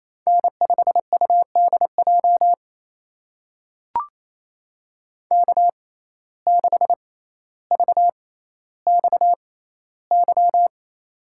Morse code, heard as a keyed tone, about 28 wpm, spaced as for 14 wpm, 700 Hz. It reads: N5UDJ K6VXY
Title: USA Call Signs - 5 Characters - 6x Character Spacing - 28wpm